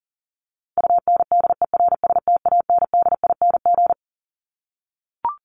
Morse code, heard as a keyed tone, about 40 wpm, 700 Hz. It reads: UNDERSTANDING
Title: ING Suffix - Top 100 Words with Base Words 40wpm